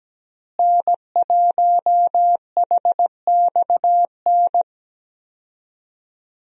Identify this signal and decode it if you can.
Morse code, heard as a keyed tone, about 17 wpm, 700 Hz. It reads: N1HXN
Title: USA Call Signs - 5 Characters - Repeated 3x - 17wpm